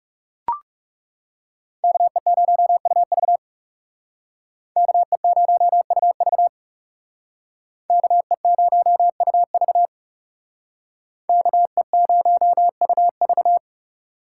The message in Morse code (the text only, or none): KE0UV KE0UV KE0UV KE0UV